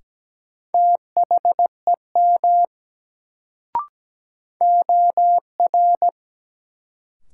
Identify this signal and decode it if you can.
Morse code, heard as a keyed tone, about 17 wpm, 700 Hz. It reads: THEM OR
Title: Sets of 2 Words - New Word Drill - 17wpm